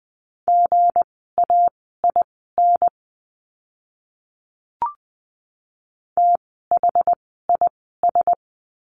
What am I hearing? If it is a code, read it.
Morse code, heard as a keyed tone, about 20 wpm, spaced as for 14 wpm, 700 Hz. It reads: GAIN THIS
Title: 160 Lesson 26f - Letter g - Words 20wpm